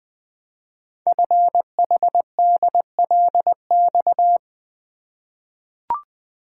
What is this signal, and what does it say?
Morse code, heard as a keyed tone, about 20 wpm, 700 Hz. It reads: FHDLX
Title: Code Groups - 5 Letter:Numbers - 20wpm